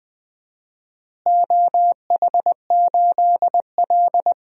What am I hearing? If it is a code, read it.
Morse code, heard as a keyed tone, about 21 wpm, 700 Hz. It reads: OH8L
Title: DXCC-100 Call Signs - 4 Characters - Repeated 2x - 20wpm